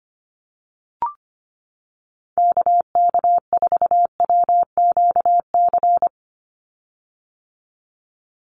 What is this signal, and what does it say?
Morse code, heard as a keyed tone, about 25 wpm, 700 Hz. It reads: KK4WQC
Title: USA Call Signs 25wpm